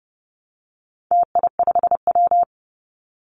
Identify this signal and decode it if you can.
Morse code, heard as a keyed tone, about 30 wpm, 700 Hz. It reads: TI5W
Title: Call Signs Repeated 2x - 30wpm